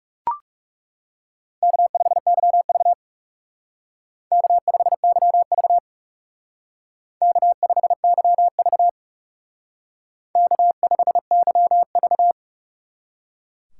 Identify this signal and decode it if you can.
Morse code, heard as a keyed tone, about 38 wpm, 700 Hz. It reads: K5YV K5YV K5YV K5YV